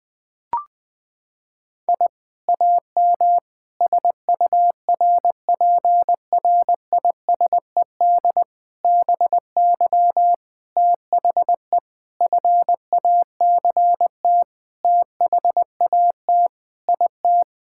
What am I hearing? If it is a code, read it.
Morse code, heard as a keyed tone, about 21 wpm, 700 Hz. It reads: I AM SURPRISED BY THE FACT THAT IT
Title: Sentences with 4-Word Phrases - 20wpm